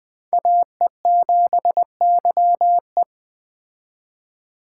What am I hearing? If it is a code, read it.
Morse code, heard as a keyed tone, about 20 wpm, 700 Hz. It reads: AE7YE
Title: Speed Racing - USA 2x2 Call Signs 17wpm